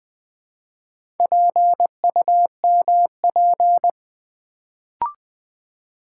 Morse code, heard as a keyed tone, about 20 wpm, 700 Hz. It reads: PUMP
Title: Build A Word - 1 Letter At A Time - 3 to 5-Letter Words - 20wpm